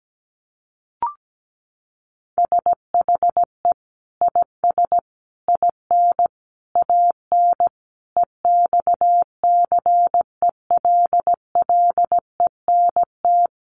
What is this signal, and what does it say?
Morse code, heard as a keyed tone, about 17 wpm, 700 Hz. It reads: SHE IS IN AN EXCELLENT
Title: Sentences from Top 1000 Words 17wpm